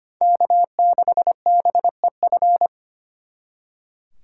Morse code, heard as a keyed tone, about 33 wpm, 700 Hz. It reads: K6BEF